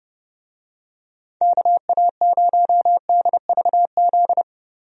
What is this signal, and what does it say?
Morse code, heard as a keyed tone, about 30 wpm, 700 Hz. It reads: KA0DVZ
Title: USA Call Signs - 6 Characters - 30wpm